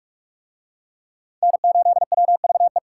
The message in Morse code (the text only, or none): N8WVE